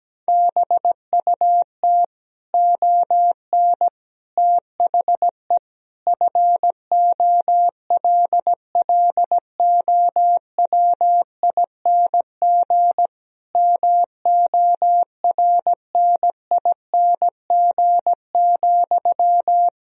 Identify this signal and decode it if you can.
Morse code, heard as a keyed tone, about 17 wpm, 700 Hz. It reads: BUT ON THE FOLLOWING MORNING,
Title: Sentences from Top 1000 Words - Review Part 4 - 17wpm